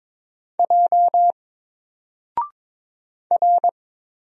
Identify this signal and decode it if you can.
Morse code, heard as a keyed tone, about 22 wpm, 700 Hz. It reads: J R
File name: Single-Letter-Number-22wpm